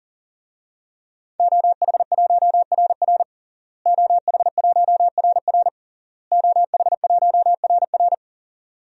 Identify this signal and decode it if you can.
Morse code, heard as a keyed tone, about 40 wpm, 700 Hz. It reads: OH1RR OH1RR OH1RR
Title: DXCC-100 Call Signs - 5 Characters - Repeated 3x - 40wpm